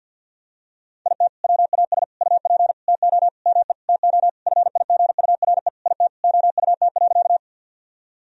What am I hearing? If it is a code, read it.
Morse code, heard as a keyed tone, about 49 wpm, 700 Hz. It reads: IT WAS UP TO ME TO FIGURE IT OUT.
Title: Sentences from Top 1000 Words - Review Part 2 - 50wpm